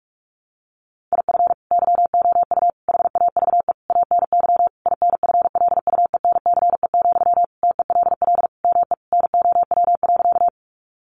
Morse code, heard as a keyed tone, about 44 wpm, 700 Hz. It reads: IF YOU HAVE ANY INFLUENCE, TELL ME NOW.
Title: Sentences from Top 1100 Words 45wpm